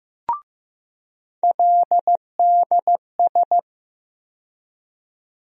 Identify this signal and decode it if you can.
Morse code, heard as a keyed tone, about 15 wpm, 700 Hz. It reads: LDS
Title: USA Call Sign Postfixes - 3 Letters 15wpm